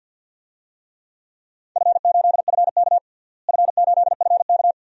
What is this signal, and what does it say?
Morse code, heard as a keyed tone, about 50 wpm, 700 Hz. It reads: F8FK F8FK
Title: DXCC-100 Call Signs - 4 Characters - Repeated 2x - 50wpm